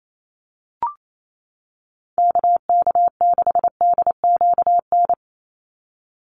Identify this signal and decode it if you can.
Morse code, heard as a keyed tone, about 28 wpm, 700 Hz. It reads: KK6DQN
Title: USA Call Signs 28wpm